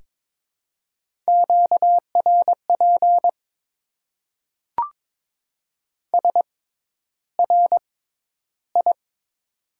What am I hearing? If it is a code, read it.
Morse code, heard as a keyed tone, about 22 wpm, spaced as for 11 wpm, 700 Hz. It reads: QRP SRI
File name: Top 90 QSO Elements - 6x Character Spacing - 22wpm